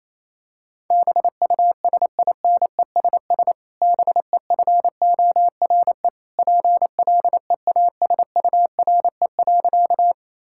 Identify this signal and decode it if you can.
Morse code, heard as a keyed tone, about 28 wpm, 700 Hz. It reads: BUSINESS BEFORE PLEASURE.
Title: Sentences from Top 1600 Words 28wpm